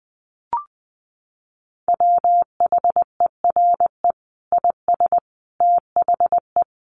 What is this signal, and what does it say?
Morse code, heard as a keyed tone, about 20 wpm, 700 Hz. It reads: WHERE IS THE
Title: Sentences from Top 1500 Words 20wpm